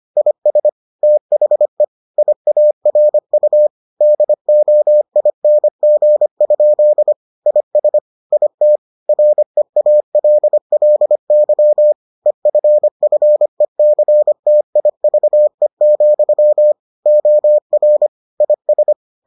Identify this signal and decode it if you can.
Morse code, heard as a keyed tone, about 25 wpm, 600 Hz. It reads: IS THE IARU DOING? IS IT REALLY EFFECTIVE, OR IS